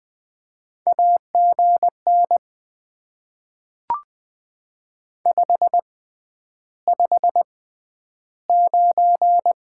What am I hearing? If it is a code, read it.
Morse code, heard as a keyed tone, about 20 wpm, spaced as for 10 wpm, 700 Hz. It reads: AGN 559
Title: Top 90 QSO Elements - 6x Character Spacing - 20wpm